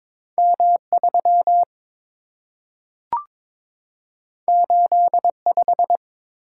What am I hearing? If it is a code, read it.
Morse code, heard as a keyed tone, about 22 wpm, 700 Hz. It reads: M3 85
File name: practice / 22wpm / Two Letter-Number Combinations - 22wpm